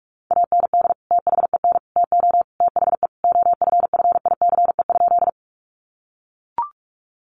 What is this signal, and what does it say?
Morse code, heard as a keyed tone, about 44 wpm, 700 Hz. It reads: AND THEN TO THE OFFICE?
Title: Sentences from Top 1000 Words 45wpm